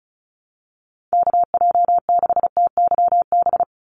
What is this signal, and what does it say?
Morse code, heard as a keyed tone, about 34 wpm, 700 Hz. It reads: KJ6TYB
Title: USA Call Signs 35wpm